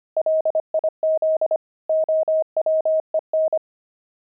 Morse code, heard as a keyed tone, about 25 wpm, 630 Hz. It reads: LIZ OWEN
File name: pangram-fs-25-cs-25-quit